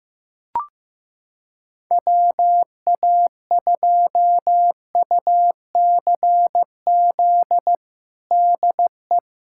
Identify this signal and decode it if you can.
Morse code, heard as a keyed tone, about 15 wpm, 700 Hz. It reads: WA2UCZ DE
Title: USA Callsign DE Callsign - 6 Characters - 15wpm